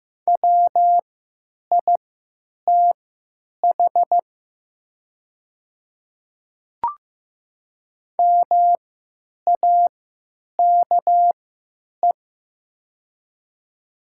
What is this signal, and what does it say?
Morse code, heard as a keyed tone, about 15 wpm, spaced as for 8 wpm, 700 Hz. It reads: WITH MAKE
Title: 3x Character Spacing - Top 100 Words - 15wpm